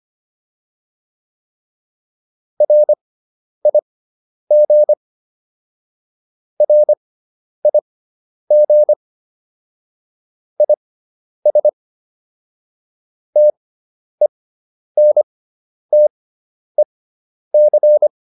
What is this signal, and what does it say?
Morse code, heard as a keyed tone, about 25 wpm, spaced as for 10 wpm, 600 Hz. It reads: RIG RIG IS TENTEC
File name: QSO402 10